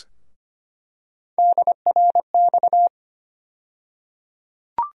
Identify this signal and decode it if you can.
Morse code, heard as a keyed tone, about 25 wpm, 700 Hz. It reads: DRX
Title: USA Call Sign Postfixes - 3 Letters 25wpm